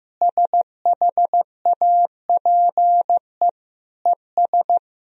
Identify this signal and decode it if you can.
Morse code, heard as a keyed tone, about 16 wpm, 700 Hz. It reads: SHAPE ES
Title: Sentences from Top 500 Words and Abbreviations 15wpm